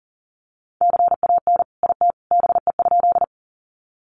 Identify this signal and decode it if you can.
Morse code, heard as a keyed tone, about 40 wpm, 700 Hz. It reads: CAN IT BE?
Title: Sentences from Top 100 Words 40wpm